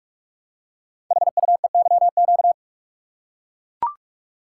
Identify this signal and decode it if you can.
Morse code, heard as a keyed tone, about 44 wpm, 700 Hz. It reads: HUEYQ
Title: Code Groups - 5 Letters - 45wpm